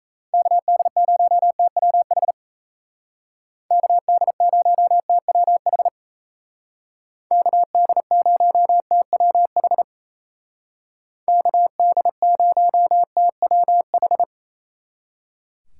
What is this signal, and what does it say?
Morse code, heard as a keyed tone, about 36 wpm, 700 Hz. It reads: KD0TWH KD0TWH KD0TWH KD0TWH